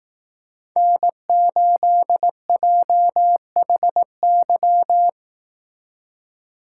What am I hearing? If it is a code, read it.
Morse code, heard as a keyed tone, about 18 wpm, 700 Hz. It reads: N8JHY
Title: Speed Racing - USA 1x3 Call Signs 15wpm